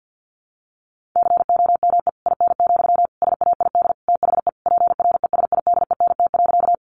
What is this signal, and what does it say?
Morse code, heard as a keyed tone, about 50 wpm, 700 Hz. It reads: COME IN, SAID THE PRESIDENT.
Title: Sentences from Top 1000 Words 50wpm